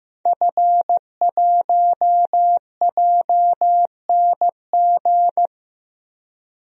Morse code, heard as a keyed tone, about 15 wpm, 700 Hz. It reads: F1JNG
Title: DXCC-100 Call Signs - 5 Characters - 15wpm